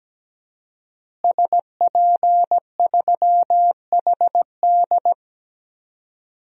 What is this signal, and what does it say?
Morse code, heard as a keyed tone, about 17 wpm, 700 Hz. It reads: SP3HD
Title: DXCC-100 Call Signs - 5 Characters - Repeated 3x - 17wpm